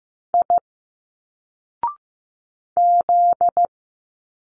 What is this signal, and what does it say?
Morse code, heard as a keyed tone, about 15 wpm, 700 Hz. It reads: I Z
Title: Single Letters 15wpm